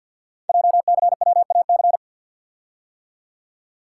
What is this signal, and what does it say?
Morse code, heard as a keyed tone, about 48 wpm, 700 Hz. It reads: JCWAC
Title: Code Groups - 5 Letters - 50wpm